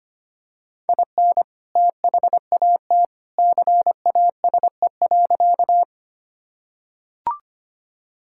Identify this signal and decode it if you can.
Morse code, heard as a keyed tone, about 25 wpm, 700 Hz. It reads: IN THAT CASE.
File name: practice / 25wpm / Sentences with 2-Word Phrases - 25wpm